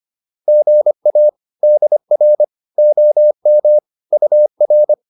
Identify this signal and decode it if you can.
Morse code, heard as a keyed tone, about 25 wpm, 600 Hz. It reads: GA DR OM UR
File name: QSO205 25